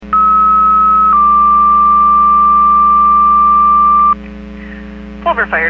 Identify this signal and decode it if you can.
Two-tone page, heard as a one-second tone, then a three-second tone, about 1290 then 1180 Hz, with speech. It is heard from 0.1 to 4.1 s.